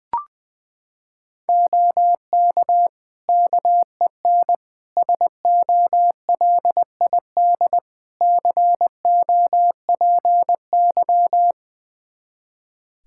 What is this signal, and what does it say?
Morse code, heard as a keyed tone, about 20 wpm, 700 Hz. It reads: OK KEN SOLID COPY
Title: Practice QSO - Exchange 3 - More Info - 20wpm